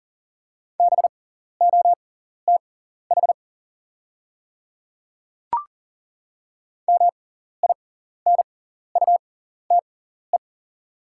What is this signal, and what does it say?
Morse code, heard as a keyed tone, about 40 wpm, spaced as for 14 wpm, 690 Hz. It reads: BOTH MINUTE
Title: 6x Character Spacing - Top 500 Words - 40wpm